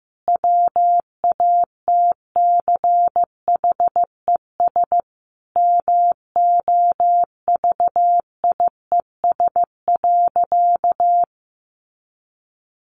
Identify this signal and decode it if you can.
Morse code, heard as a keyed tone, about 15 wpm, 700 Hz. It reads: WATCHES MOVIES.
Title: Additional Easy Phrases 15wpm